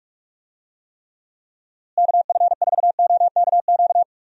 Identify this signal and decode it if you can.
Morse code, heard as a keyed tone, about 46 wpm, 700 Hz. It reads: KF4OKQ
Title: USA Call Signs - 6 Characters - Repeated 3x - 45wpm